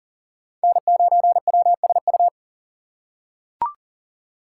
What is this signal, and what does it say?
Morse code, heard as a keyed tone, about 40 wpm, 700 Hz. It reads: N9WSU
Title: USA Call Signs - 5 Characters - 3x Character Spacing - 40wpm